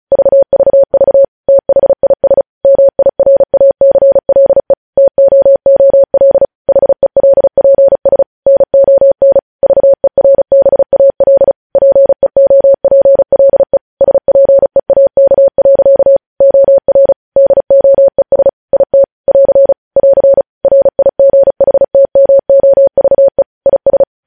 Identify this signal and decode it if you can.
Morse code, heard as a keyed tone, about 36 wpm, 550 Hz. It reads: VVV THIS MIRACLE TOOL HELPS NON VERBAL PEOPLE SPEAK. OR DOES IT <AR> <AR> RIGHTMOVE IS